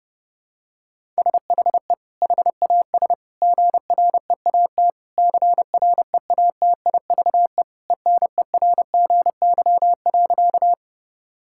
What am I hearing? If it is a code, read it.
Morse code, heard as a keyed tone, about 30 wpm, 700 Hz. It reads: SHE HAS GREAT CREATIVE ENERGY.
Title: Sentences from Top 1000 Words - Review Part 6 - 30wpm